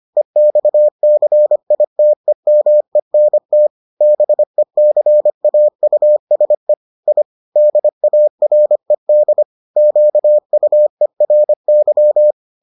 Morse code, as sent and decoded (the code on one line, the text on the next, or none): . -..- -.-. .. - . -- . -. - / -... -.-. .- ..- ... / .. / -.. .- .-. -.. / --.- ..- . .-. -.--
EXCITEMENT BECAUSE I DARED QUERY